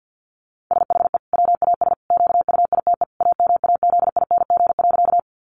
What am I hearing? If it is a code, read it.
Morse code, heard as a keyed tone, about 50 wpm, 700 Hz. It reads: SHE WAS QUITE AMAZING.
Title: Sentences from Top 1500 Words 50wpm